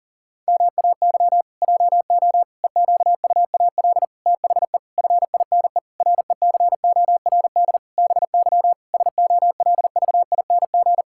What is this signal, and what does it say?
Morse code, heard as a keyed tone, about 37 wpm, 700 Hz. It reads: MAY JO EQUAL THE FINE RECORD BY SOLVING
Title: Pangram Challenge v2 - 40wpm